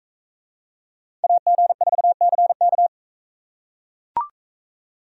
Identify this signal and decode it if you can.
Morse code, heard as a keyed tone, about 43 wpm, 700 Hz. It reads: AG4CK